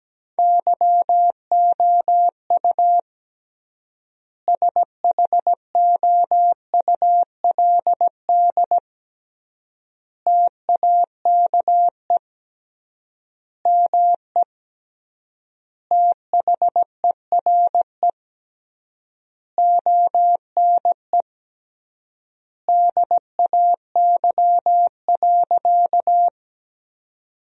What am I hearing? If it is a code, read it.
Morse code, heard as a keyed tone, about 17 wpm, 700 Hz. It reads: YOU SHOULD TAKE ME THERE ONE DAY.
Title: 3x Word-Spacing - Sentences from Top 300 Words - 17wpm